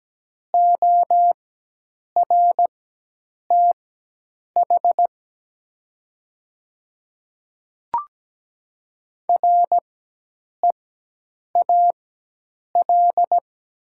Morse code, heard as a keyed tone, about 17 wpm, spaced as for 8 wpm, 700 Hz. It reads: ORTH REAL
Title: 4x Character Spacing - Top 300 Words - 17wpm